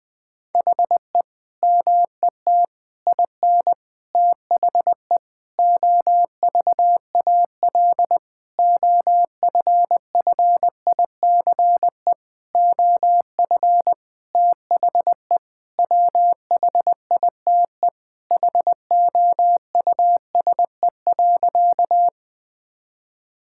Morse code, heard as a keyed tone, about 20 wpm, 700 Hz. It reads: HE MET IN THE OVAL OFFICE OF THE WHITE HOUSE.
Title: Sentences with 8-Word Phrases - 20wpm